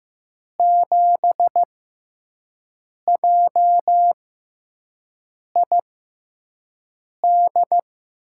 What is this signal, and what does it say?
Morse code, heard as a keyed tone, about 15 wpm, spaced as for 5 wpm, 700 Hz. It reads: 7JID
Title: USA Call Signs - 6x Character Spacing - 15wpm